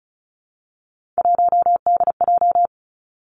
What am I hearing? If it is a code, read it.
Morse code, heard as a keyed tone, about 36 wpm, 700 Hz. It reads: 1DJ
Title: Three Challenging Letter-Number Combinations 35wpm